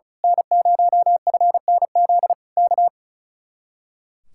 Morse code, heard as a keyed tone, about 32 wpm, 700 Hz. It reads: N0FNZ K
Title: Speed-Racing Common QSO Phrases - 35wpm